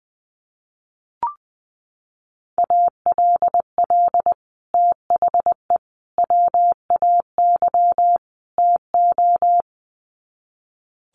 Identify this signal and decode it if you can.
Morse code, heard as a keyed tone, about 20 wpm, 700 Hz. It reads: ALL THE WAY TO
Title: Sets of 4 Words 20wpm